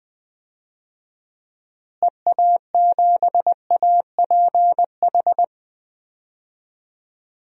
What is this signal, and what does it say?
Morse code, heard as a keyed tone, about 21 wpm, 700 Hz. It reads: EA7APH
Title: DXCC-100 Call Signs - 6 Characters - Repeated 3x - 20wpm